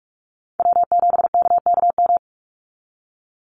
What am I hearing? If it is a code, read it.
Morse code, heard as a keyed tone, about 44 wpm, 700 Hz. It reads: W7KKM
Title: NA SOTA Chaser Callsigns 45wpm